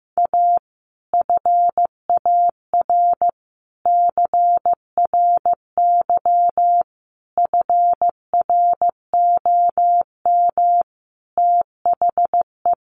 Morse code, heard as a keyed tone, about 15 wpm, 700 Hz. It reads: A FAR CRY FROM THE
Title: Sets of 5 Words Encore 15wpm